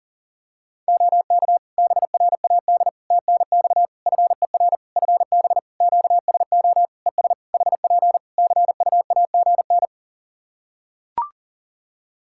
Morse code, heard as a keyed tone, about 40 wpm, 700 Hz. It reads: OK BRAD TNX FER FB QSO ES HP CUAGN
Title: K1USN SST QSO Practice 40wpm